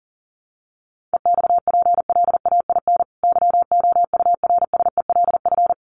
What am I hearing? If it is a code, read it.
Morse code, heard as a keyed tone, about 40 wpm, 700 Hz. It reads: EXPLAIN YOURSELF
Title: Sentences from Top 1300 Words 40wpm